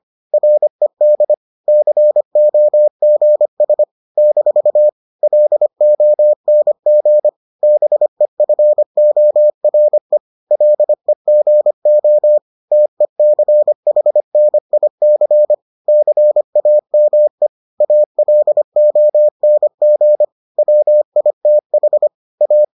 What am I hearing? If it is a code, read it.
Morse code, heard as a keyed tone, about 25 wpm, 600 Hz. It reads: RED COGS - LONG BEFORE LEGO TECHNIC CAME ALONG WITH A